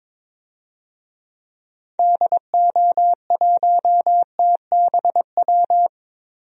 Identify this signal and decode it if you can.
Morse code, heard as a keyed tone, about 22 wpm, 700 Hz. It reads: DO1TBW
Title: DXCC-100 Call Signs - 6 Characters - Repeated 2x - 22wpm